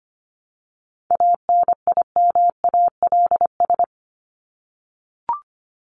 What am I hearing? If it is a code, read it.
Morse code, heard as a keyed tone, about 25 wpm, 700 Hz. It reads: ANIMALS
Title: ES:S Suffix - Top 100 Words with Base Words 25wpm